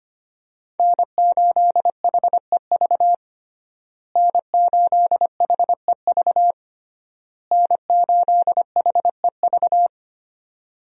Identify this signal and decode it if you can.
Morse code, heard as a keyed tone, about 25 wpm, 700 Hz. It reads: N8HEV N8HEV N8HEV
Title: USA Call Signs - 5 Characters - Repeated 3x - 25wpm